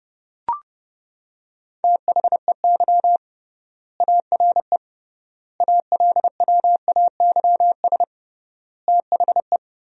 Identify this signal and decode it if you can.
Morse code, heard as a keyed tone, about 30 wpm, 700 Hz. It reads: THEY ARE ALWAYS THE
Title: 3x Word-Spacing - Sentences from Top 300 Words - 30wpm